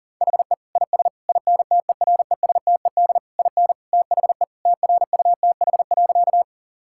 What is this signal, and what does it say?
Morse code, heard as a keyed tone, about 40 wpm, 700 Hz. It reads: HE IS INTERESTED IN THE TRUTH.
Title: Sentences from Top 1000 Words - Review Part 6 - 40wpm